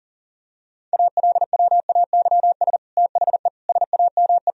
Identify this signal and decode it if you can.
Morse code, heard as a keyed tone, about 40 wpm, 700 Hz. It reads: ALWAYS THE SAME
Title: Sentences from Top 1000 Words - Review Part 1 - 40wpm